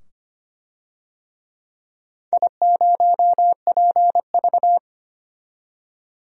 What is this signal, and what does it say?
Morse code, heard as a keyed tone, about 25 wpm, 700 Hz. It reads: I0PV
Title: DXCC-100 Call Signs - 4 Characters - 25wpm